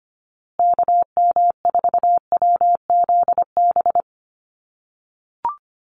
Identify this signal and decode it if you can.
Morse code, heard as a keyed tone, about 25 wpm, 700 Hz. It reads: KM4WZB
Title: USA Call Signs Repeated 3x 25wpm